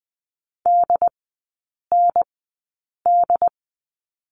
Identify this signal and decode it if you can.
Morse code, heard as a keyed tone, about 20 wpm, spaced as for 8 wpm, 700 Hz. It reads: DND
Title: 067 Lesson 12e - Letter d - Three Characters 20wpm